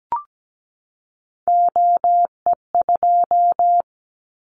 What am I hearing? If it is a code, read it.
Morse code, heard as a keyed tone, about 17 wpm, 700 Hz. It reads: OE2
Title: Callsign Prefixes for Top 25 DXCC - 17wpm